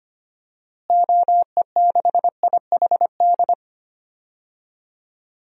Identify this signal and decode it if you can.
Morse code, heard as a keyed tone, about 25 wpm, 700 Hz. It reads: OE6IHD
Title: DXCC-100 Call Signs - 6 Characters - Repeated 2x - 25wpm